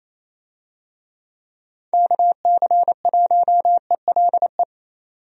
Morse code, heard as a keyed tone, about 28 wpm, 700 Hz. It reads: KC1ELE
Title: USA Call Signs - 6 Characters - 28wpm